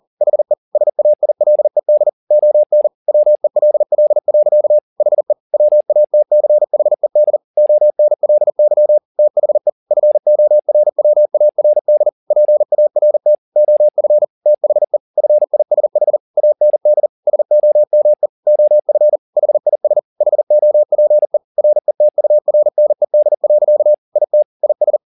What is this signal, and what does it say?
Morse code, heard as a keyed tone, about 40 wpm, 600 Hz. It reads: HE SAILED ON WELL. HE WATCHED ONLY THE FORWARD PART OF THE FISH AND SOME OF HIS HOPE RETURNED. IT IS
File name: the old man and the sea 2 40wpm